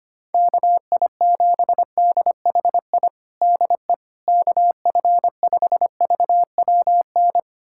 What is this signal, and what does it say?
Morse code, heard as a keyed tone, about 25 wpm, 700 Hz. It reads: KI7DHI DE KF5VWN
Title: USA Callsign DE Callsign - 6 Characters - 25wpm